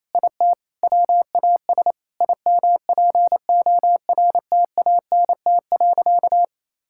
Morse code, heard as a keyed tone, about 27 wpm, 700 Hz. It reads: IT WAS IMPORTANT.
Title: Sentences from Top 1000 Words - Review Part 5 - 28wpm